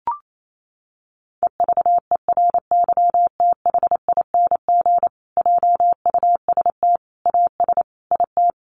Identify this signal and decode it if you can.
Morse code, heard as a keyed tone, about 28 wpm, 700 Hz. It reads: EVERYTHING JUST AS IT
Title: Sentences from Top 700 Words 28wpm